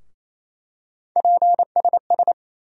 Morse code, heard as a keyed tone, about 28 wpm, 700 Hz. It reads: PSS
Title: USA Call Sign Postfixes - 3 Letters 28wpm